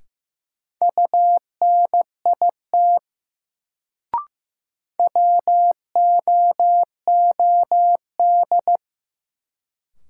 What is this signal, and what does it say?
Morse code, heard as a keyed tone, about 15 wpm, 700 Hz. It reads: UNIT WOOD
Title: Sentences from Top 1000 Words - Review Part 2 - New Word Drill - 15wpm